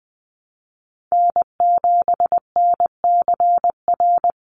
Morse code, heard as a keyed tone, about 20 wpm, 700 Hz. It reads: N7NCR
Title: USA Call Signs 20wpm